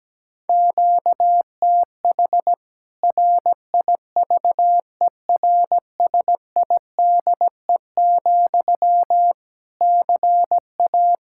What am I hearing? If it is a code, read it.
Morse code, heard as a keyed tone, about 17 wpm, 700 Hz. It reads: QTH RIVERSIDE, CA
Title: K1USN SST QSO Practice 17wpm